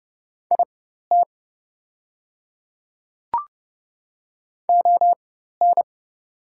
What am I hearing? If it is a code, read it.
Morse code, heard as a keyed tone, about 30 wpm, spaced as for 14 wpm, 700 Hz. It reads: IT ON
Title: 4x Character Spacing - Top 200 Words - 30wpm